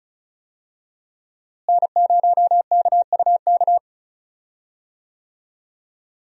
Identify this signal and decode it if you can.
Morse code, heard as a keyed tone, about 34 wpm, 700 Hz. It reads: N0KUK
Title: USA Call Signs - 5 Characters - Repeated 2x - 35wpm